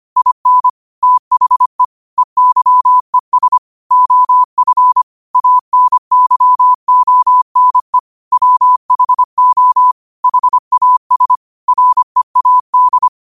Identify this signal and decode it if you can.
Morse code, heard as a keyed tone, about 25 wpm, 1000 Hz. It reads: IN THE EYES OF ANYONE WHO HAS READ
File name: qotdfiles / 25wpm